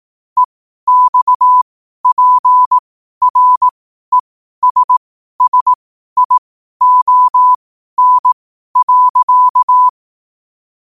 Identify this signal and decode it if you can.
Morse code, heard as a keyed tone, about 18 wpm, spaced as for 12 wpm, 1000 Hz. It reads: EXPRESSION.
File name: qotdfiles / 12wpm